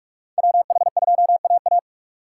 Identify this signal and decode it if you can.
Morse code, heard as a keyed tone, about 46 wpm, 700 Hz. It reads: WH2AA
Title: USA Call Signs - 5 Characters - 8x Character Spacing - 45wpm